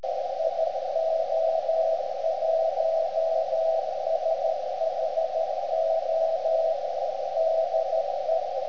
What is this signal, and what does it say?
Morse code, heard as a keyed tone, about 13 wpm, 650 Hz, with radio noise. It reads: IO1SHO DE
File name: ft8-vs-cw / 80-to-150 CW 300hz